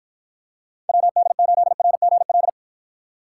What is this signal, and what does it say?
Morse code, heard as a keyed tone, about 52 wpm, 700 Hz. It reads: WD8RGL